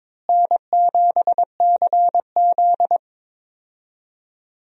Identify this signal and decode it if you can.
Morse code, heard as a keyed tone, about 22 wpm, 700 Hz. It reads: N7CZ
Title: USA Call Signs - 4 Characters - Repeated 3x - 22wpm